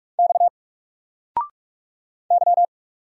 Morse code, heard as a keyed tone, about 45 wpm, 700 Hz. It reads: X Y
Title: Single Letter-Number 45wpm